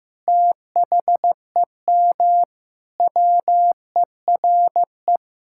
Morse code, heard as a keyed tone, about 15 wpm, 700 Hz. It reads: THEM WERE
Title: Sentences from Top 1000 Words - Review Part 6 - 15wpm